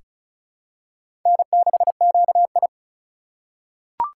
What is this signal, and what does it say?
Morse code, heard as a keyed tone, about 34 wpm, 700 Hz. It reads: N6QI
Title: USA Call Signs - 4 Characters - 2x Character Spacing - 35wpm